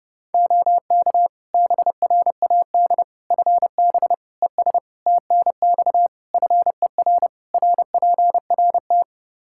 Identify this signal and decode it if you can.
Morse code, heard as a keyed tone, about 30 wpm, 700 Hz. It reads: OK BRAD FB ES TNX FER RPRT
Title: K1USN SST QSO Practice 30wpm